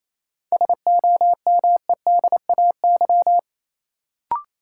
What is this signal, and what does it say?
Morse code, heard as a keyed tone, about 27 wpm, 700 Hz. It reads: SOMEDAY
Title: Sentences from Top 1600 Words - New Word Drill - 28wpm